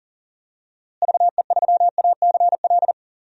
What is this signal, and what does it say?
Morse code, heard as a keyed tone, about 37 wpm, 700 Hz. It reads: VE3ACL
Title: DXCC-100 Call Signs - 6 Characters - Repeated 3x - 40wpm